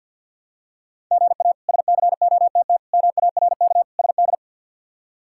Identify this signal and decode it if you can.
Morse code, heard as a keyed tone, about 49 wpm, 700 Hz. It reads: GA SCOTT MARK SD
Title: K1USN SST QSO Practice - 50wpm